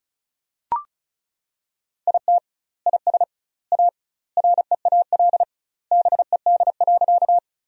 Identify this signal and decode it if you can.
Morse code, heard as a keyed tone, about 34 wpm, 700 Hz. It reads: IT IS A REAL BED.
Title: 2x Word-Spacing - Sentences from Top 500 Words - 35wpm